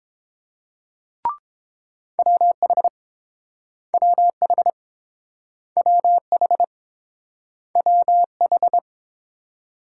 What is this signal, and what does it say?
Morse code, heard as a keyed tone, about 30 wpm, 700 Hz. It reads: WH WH WH WH